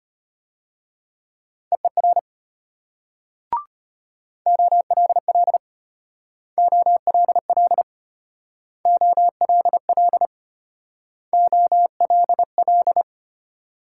Morse code, heard as a keyed tone, about 33 wpm, 700 Hz. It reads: EER OLL OLL OLL OLL